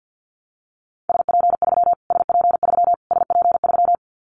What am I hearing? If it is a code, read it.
Morse code, heard as a keyed tone, about 50 wpm, 700 Hz. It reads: SP3 SP3 SP3
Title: Callsign Prefixes for Top 25 DXCC - Repeated 3x - 50wpm